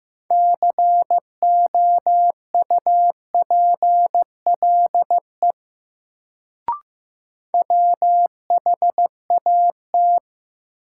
Morse code, heard as a keyed tone, about 16 wpm, 700 Hz. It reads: COUPLE WHAT
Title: Sets of 2 Words - New Word Drill - 15wpm